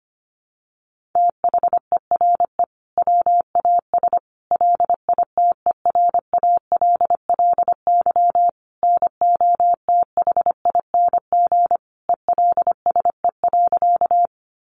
Morse code, heard as a keyed tone, about 25 wpm, 700 Hz. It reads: THERE WAS LITERALLY NOTHING ELSE.
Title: Sentences from Top 1400 Words 25wpm